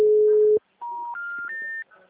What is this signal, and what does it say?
Special information tone.